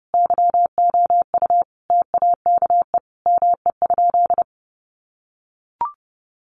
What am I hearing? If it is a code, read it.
Morse code, heard as a keyed tone, about 30 wpm, 700 Hz. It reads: YOU TAKE ME?
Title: Sentences from Top 200 Words 30wpm